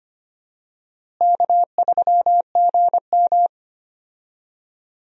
Morse code, heard as a keyed tone, about 25 wpm, 700 Hz. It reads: K3GM
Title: USA Call Signs - 4 Characters - Repeated 2x - 25wpm